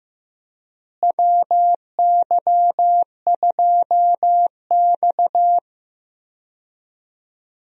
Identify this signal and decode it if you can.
Morse code, heard as a keyed tone, about 15 wpm, 700 Hz. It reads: WY2X
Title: USA Call Signs - 4 Characters - Repeated 2x - 15wpm